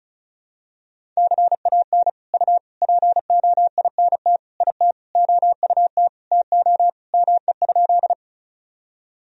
Morse code, heard as a keyed tone, about 36 wpm, 700 Hz. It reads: CAN U POINT IT OUT TO ME?
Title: Sentences from Top 200 Words and Abbreviations 35wpm